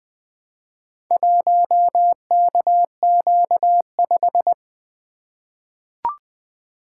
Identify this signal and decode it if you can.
Morse code, heard as a keyed tone, about 20 wpm, 700 Hz. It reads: 1KQ5
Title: Four Letter-Number Combinations - 20wpm